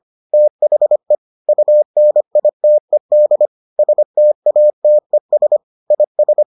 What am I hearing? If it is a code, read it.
Morse code, heard as a keyed tone, about 25 wpm, 600 Hz. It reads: THE UNITED STATES IS